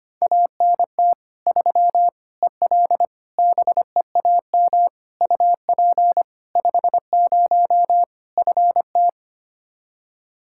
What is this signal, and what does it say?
Morse code, heard as a keyed tone, about 25 wpm, 700 Hz. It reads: ANT 3 EL BEAM UP 50 FT
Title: Practice QSO - Exchange 2 - Info - 25wpm